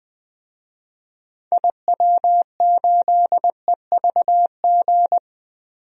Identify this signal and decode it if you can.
Morse code, heard as a keyed tone, about 20 wpm, 700 Hz. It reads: IW8EVG
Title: DXCC-100 Call Signs - 6 Characters - Repeated 2x - 20wpm